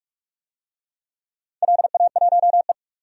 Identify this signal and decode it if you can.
Morse code, heard as a keyed tone, about 44 wpm, 700 Hz. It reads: LA1E
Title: DXCC-100 Call Signs - 4 Characters - 45wpm